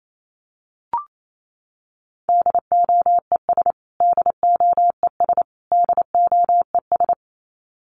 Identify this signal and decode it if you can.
Morse code, heard as a keyed tone, about 28 wpm, 700 Hz. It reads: DOES DOES DOES
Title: Top 300 Words Repeated 28wpm